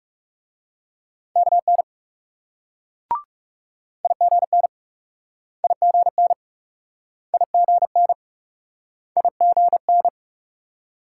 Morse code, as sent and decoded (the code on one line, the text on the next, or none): -.- -. / .. --. -. / .. --. -. / .. --. -. / .. --. -.
KN IGN IGN IGN IGN